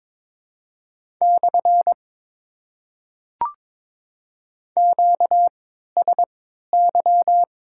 Morse code, heard as a keyed tone, about 21 wpm, spaced as for 6 wpm, 690 Hz. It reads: /QSY
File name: Top 90 QSO Elements - 3x Character Spacing - 22wpm